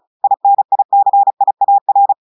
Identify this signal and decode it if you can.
Morse code, heard as a keyed tone, about 36 wpm, 800 Hz. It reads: INICIAR
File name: portugais texte en clair 2 vitesse 35 mots minutes